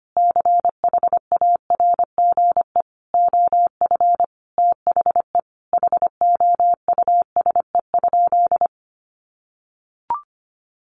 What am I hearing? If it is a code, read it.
Morse code, heard as a keyed tone, about 25 wpm, 700 Hz. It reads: CHARGE OF THE HOUSE?
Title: Sentences from Top 900 Words 25wpm